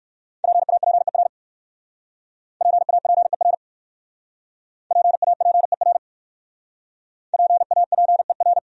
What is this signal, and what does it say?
Morse code, heard as a keyed tone, about 57 wpm, 700 Hz. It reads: PAPER PAPER PAPER PAPER